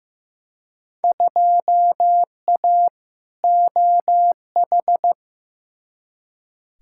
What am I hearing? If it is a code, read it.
Morse code, heard as a keyed tone, about 15 wpm, 700 Hz. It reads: 2A OH
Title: Field Day Exchanges - 15wpm